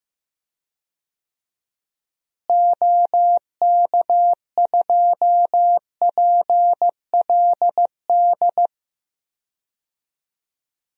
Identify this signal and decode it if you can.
Morse code, heard as a keyed tone, about 15 wpm, 700 Hz. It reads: OK2PLD
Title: DXCC-100 Call Signs - 6 Characters - Repeated 3x - 15wpm